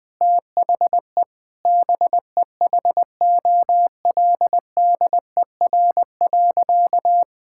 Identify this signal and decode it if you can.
Morse code, heard as a keyed tone, about 21 wpm, 700 Hz. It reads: THE BEHOLDER.
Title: Sentences with 7-Word Phrases - Spoken Beginning - 20wpm